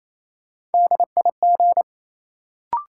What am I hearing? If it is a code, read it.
Morse code, heard as a keyed tone, about 28 wpm, 700 Hz. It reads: DIG
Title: Sentences from Top 1600 Words - New Word Drill - 28wpm